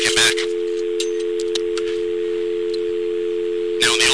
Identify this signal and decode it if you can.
Telephone call progress tone (North America), heard as a telephone network's tone, about 350, 440 Hz, with noise. Dial tone.